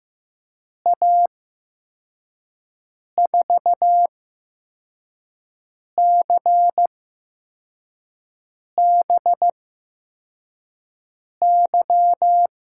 Morse code, heard as a keyed tone, about 15 wpm, spaced as for 4 wpm, 700 Hz. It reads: A4CBY
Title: USA Call Signs - 6 Characters - 8x Character Spacing - 15wpm